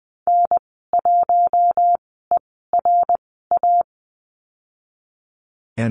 Morse code, heard as a keyed tone, about 19 wpm, spaced as for 14 wpm, 700 Hz. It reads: N1ERA
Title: 069 Lesson 12g - Letter d - Call Signs 20wpm